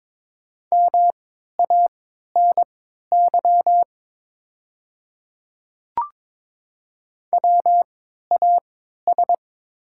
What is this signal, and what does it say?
Morse code, heard as a keyed tone, about 22 wpm, spaced as for 12 wpm, 700 Hz. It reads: MANY WAS
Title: 3x Character Spacing - Top 100 Words - 22wpm